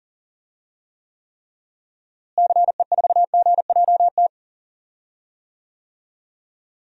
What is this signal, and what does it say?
Morse code, heard as a keyed tone, about 40 wpm, 700 Hz. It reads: CE4GJT
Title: DXCC-100 Call Signs - 6 Characters - Repeated 3x - 40wpm